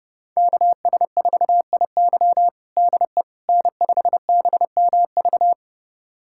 Morse code, heard as a keyed tone, about 30 wpm, 700 Hz. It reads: KS4IY DE N5BMV
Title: USA Callsign DE Callsign - 5 Characters - 30wpm